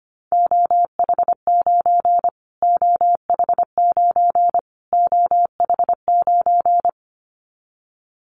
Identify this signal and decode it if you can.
Morse code, heard as a keyed tone, about 25 wpm, 700 Hz. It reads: OH9 OH9 OH9
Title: Callsign Prefixes for Top 25 DXCC - Repeated 3x - 25wpm